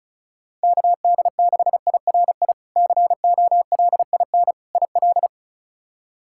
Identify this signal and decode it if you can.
Morse code, heard as a keyed tone, about 34 wpm, 700 Hz. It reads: KD6IRI COLIN IL
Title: K1USN SST QSO Practice - 35wpm